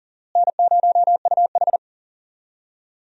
Morse code, heard as a keyed tone, about 37 wpm, 700 Hz. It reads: N0UH
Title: USA Call Signs - 4 Characters - Repeated 2x - 40wpm